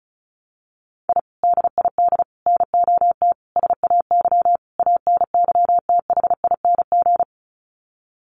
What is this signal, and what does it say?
Morse code, heard as a keyed tone, about 34 wpm, 700 Hz. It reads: I DID NOT SAY ANYTHING
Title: Sets of 5 Words Encore2 35wpm